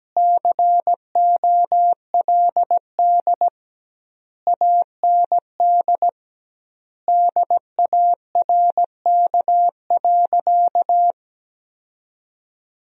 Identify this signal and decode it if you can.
Morse code, heard as a keyed tone, about 17 wpm, 700 Hz. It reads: COLD AND DARK.
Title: 2x Word-Spacing - Sentences from Top 500 Words - 17wpm